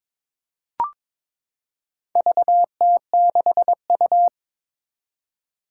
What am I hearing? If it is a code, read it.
Morse code, heard as a keyed tone, about 22 wpm, 700 Hz. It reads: VT6U
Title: Four Challenging Letter-Number Combinations 22wpm